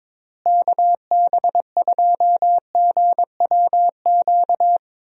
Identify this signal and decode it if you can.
Morse code, heard as a keyed tone, about 22 wpm, 700 Hz. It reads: KB2GWQ
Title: USA Call Signs - 6 Characters - Repeated 3x - 22wpm